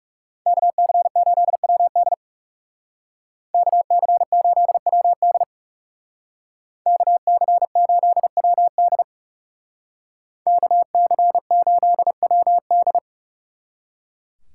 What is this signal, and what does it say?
Morse code, heard as a keyed tone, about 36 wpm, 700 Hz. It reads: KC8WD KC8WD KC8WD KC8WD
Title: Speed Racing - USA 2x2 Call Signs 30wpm